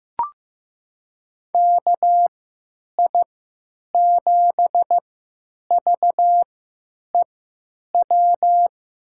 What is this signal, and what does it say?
Morse code, heard as a keyed tone, about 15 wpm, spaced as for 9 wpm, 700 Hz. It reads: KI7VEW
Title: USA Call Signs - 6 Characters - 3x Character Spacing - 15wpm